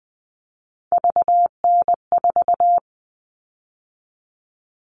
Morse code, heard as a keyed tone, about 20 wpm, 700 Hz. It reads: VN4
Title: All Cut Numbers - 3-Digit 20wpm